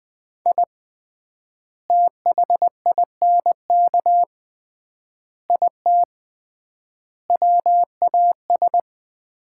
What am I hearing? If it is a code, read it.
Morse code, heard as a keyed tone, about 20 wpm, 700 Hz. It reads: I THINK IT WAS
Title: 3x Word-Spacing - Sets of 5 Words - 20wpm